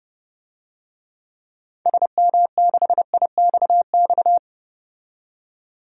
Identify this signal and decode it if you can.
Morse code, heard as a keyed tone, about 31 wpm, 700 Hz. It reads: SM6IXX
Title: DXCC-100 Call Signs - 6 Characters - 30wpm